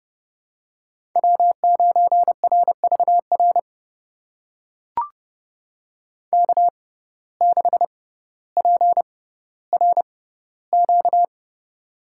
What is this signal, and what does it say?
Morse code, heard as a keyed tone, about 30 wpm, spaced as for 15 wpm, 700 Hz. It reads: W9RVR K6PRQ
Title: USA Call Signs - 5 Characters - 6x Character Spacing - 30wpm